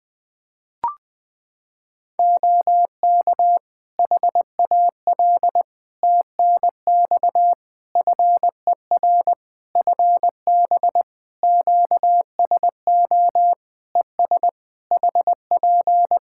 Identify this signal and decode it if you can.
Morse code, heard as a keyed tone, about 21 wpm, 700 Hz. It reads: OK HAL TNX FER FB QSO ES HP
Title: Practice QSO - Exchange 4 - Ending - 20wpm